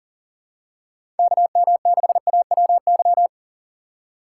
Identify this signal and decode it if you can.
Morse code, heard as a keyed tone, about 40 wpm, 690 Hz. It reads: KM6AWY
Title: USA Call Signs - 6x Character Spacing - 40wpm